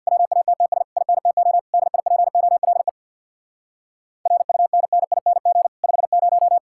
Morse code, heard as a keyed tone, about 64 wpm, 700 Hz. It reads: WATTS INTO DIPOLE RUNNING 50